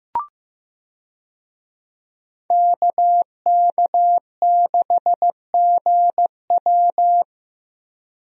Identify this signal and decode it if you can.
Morse code, heard as a keyed tone, about 15 wpm, 700 Hz. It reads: KK6GW
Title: USA Call Signs - 5 Characters - Repeated 3x - 15wpm